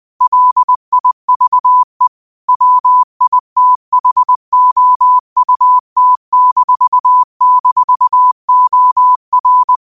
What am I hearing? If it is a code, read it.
Morse code, heard as a keyed tone, about 20 wpm, 1000 Hz. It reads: LIVE WITHOUT--OR